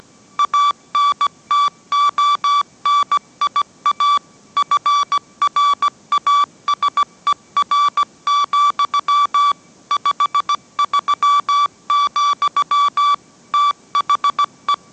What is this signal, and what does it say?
Morse code, heard as a keyed tone, about 17 wpm, 1150 Hz. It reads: ANTONIA FRASER, 53, THE